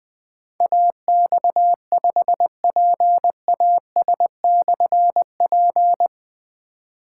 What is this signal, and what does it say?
Morse code, heard as a keyed tone, about 20 wpm, 700 Hz. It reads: AX5PAS/P
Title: Difficult Call Signs 20wpm